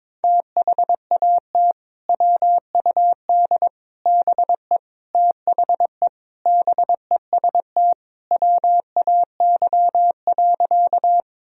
Morse code, heard as a keyed tone, about 21 wpm, 700 Hz. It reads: THAT WUD BE THE BEST WAY.
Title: Sentences from Top 400 Words and Abbreviations 22wpm